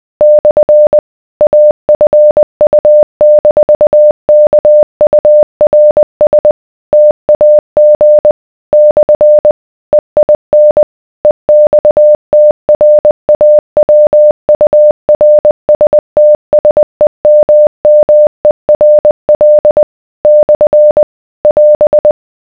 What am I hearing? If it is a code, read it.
Morse code, heard as a keyed tone, about 20 wpm, 600 Hz. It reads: / AFU-KURS TAG / EIN EXTRAWURSTSEMMERL / <AS>